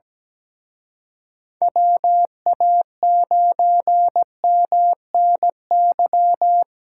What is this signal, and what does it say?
Morse code, heard as a keyed tone, about 20 wpm, 700 Hz. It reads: WA9MNY
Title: Speed Racing - USA 2x3 Call Signs 17wpm